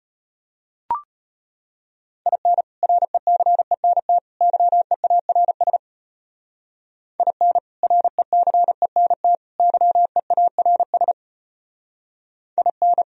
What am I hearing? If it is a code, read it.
Morse code, heard as a keyed tone, about 32 wpm, 700 Hz. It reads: IN RECENT YEARS IN RECENT YEARS IN